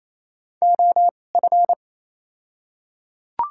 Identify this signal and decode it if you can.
Morse code, heard as a keyed tone, about 28 wpm, spaced as for 20 wpm, 700 Hz. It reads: OF
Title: 2x Character Spacing - Top 300 Words - 28wpm